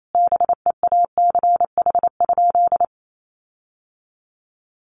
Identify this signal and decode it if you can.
Morse code, heard as a keyed tone, about 28 wpm, 700 Hz. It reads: BEACH?
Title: Sentences from Top 1500 Words 28wpm